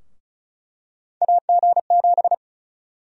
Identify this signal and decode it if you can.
Morse code, heard as a keyed tone, about 34 wpm, 700 Hz. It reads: AG7
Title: USA Call Sign Prefixes - Repeated 2x - 35wpm